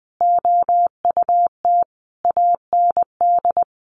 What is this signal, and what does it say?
Morse code, heard as a keyed tone, about 20 wpm, 700 Hz. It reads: OUT AND
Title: Sentences from Top 1400 Words 20wpm